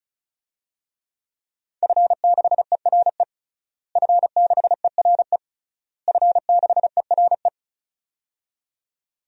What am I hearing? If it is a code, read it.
Morse code, heard as a keyed tone, about 36 wpm, 700 Hz. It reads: F6ERE F6ERE F6ERE
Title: DXCC-100 Call Signs - 5 Characters - Repeated 3x - 35wpm